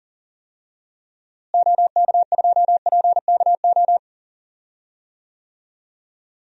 Morse code, heard as a keyed tone, about 39 wpm, 700 Hz. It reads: OK2PKO
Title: DXCC-100 Call Signs - 6 Characters - Repeated 2x - 40wpm